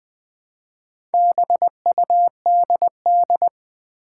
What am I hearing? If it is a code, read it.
Morse code, heard as a keyed tone, about 20 wpm, 700 Hz. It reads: BUDD
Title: Build A Word - 1 Letter At A Time - 3 to 5-Letter Words - 20wpm